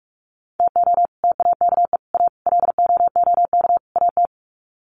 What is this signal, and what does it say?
Morse code, heard as a keyed tone, about 44 wpm, 700 Hz. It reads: TO TAKE A LOOK AT
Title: Sets of 5 Words 45wpm